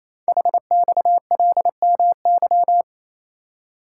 Morse code, heard as a keyed tone, about 27 wpm, 700 Hz. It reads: HXLMY
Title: Code Groups - 5 Letters - 28wpm